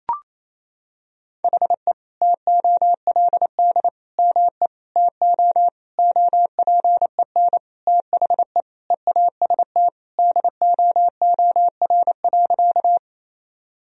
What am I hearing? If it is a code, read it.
Morse code, heard as a keyed tone, about 28 wpm, 700 Hz. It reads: HE TOLD ME TO OPEN THE EAST DOOR.
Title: Sentences from Top 1000 Words - Review Part 3 - 28wpm